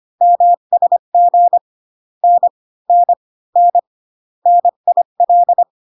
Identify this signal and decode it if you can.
Morse code, heard as a keyed tone, about 25 wpm, 700 Hz. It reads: MSG N N N NIL